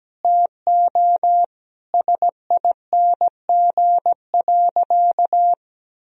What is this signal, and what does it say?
Morse code, heard as a keyed tone, about 17 wpm, 700 Hz. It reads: TO SING.
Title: Sentences from Top 1000 Words - Review Part 2 - 17wpm